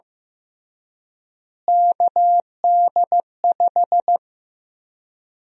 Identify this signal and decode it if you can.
Morse code, heard as a keyed tone, about 16 wpm, 700 Hz. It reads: KD5
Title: USA Call Sign Prefixes - Repeated 2x - 15wpm